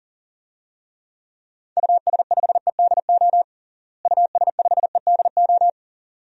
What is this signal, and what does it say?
Morse code, heard as a keyed tone, about 40 wpm, 700 Hz. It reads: US5EDO US5EDO
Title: DXCC-100 Call Signs - 6 Characters - Repeated 2x - 40wpm